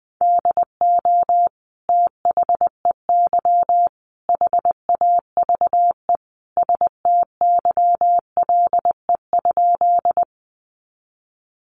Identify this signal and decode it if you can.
Morse code, heard as a keyed tone, about 20 wpm, 700 Hz. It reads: DO THEY HAVE STYLE?
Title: Sentences from Top 800 Words 20wpm